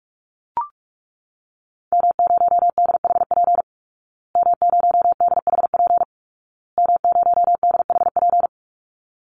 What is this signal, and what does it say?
Morse code, heard as a keyed tone, about 44 wpm, 700 Hz. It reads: M0DHP M0DHP M0DHP
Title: Call Signs Repeated 2x - 45wpm